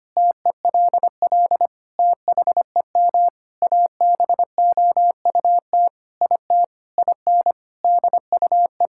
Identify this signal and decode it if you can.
Morse code, heard as a keyed tone, about 25 wpm, 700 Hz. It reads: TELL THEM ABOUT IT IN DUE
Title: Sentences from Top 1000 Words - Review Part 6 - 25wpm